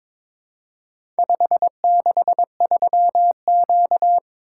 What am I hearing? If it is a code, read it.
Morse code, heard as a keyed tone, about 21 wpm, 700 Hz. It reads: 563Q
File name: practice / 22wpm / Four Letter-Number Combinations - 22wpm